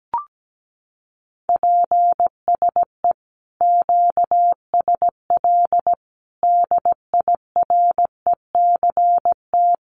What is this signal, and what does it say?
Morse code, heard as a keyed tone, about 17 wpm, 700 Hz. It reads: PSE QSL DIRECT
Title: Common QSO Phrases 17wpm